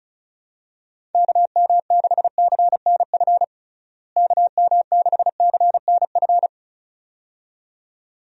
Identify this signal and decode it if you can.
Morse code, heard as a keyed tone, about 34 wpm, 700 Hz. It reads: KM6CNF KM6CNF
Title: USA Call Signs - 6 Characters - Repeated 2x - 35wpm